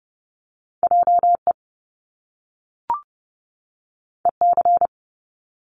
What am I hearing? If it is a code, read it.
Morse code, heard as a keyed tone, about 30 wpm, 700 Hz. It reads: JE EC
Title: Top Two Letter Combinations in Words 30wpm